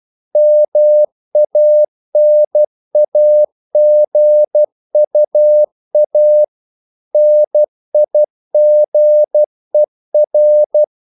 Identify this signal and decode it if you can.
Morse code, heard as a keyed tone, about 12 wpm, 600 Hz. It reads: MANAGUA NIGER
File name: CW / Practice / NationalCapitals12